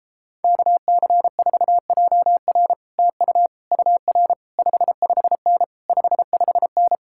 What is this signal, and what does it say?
Morse code, heard as a keyed tone, about 33 wpm, 700 Hz. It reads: KC4JR TU UR 55N 55N